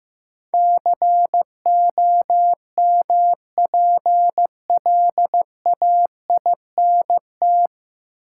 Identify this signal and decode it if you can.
Morse code, heard as a keyed tone, about 16 wpm, 700 Hz. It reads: COMPLAINT
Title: Sentences from Top 1400 Words - New Word Drill Alt - 15wpm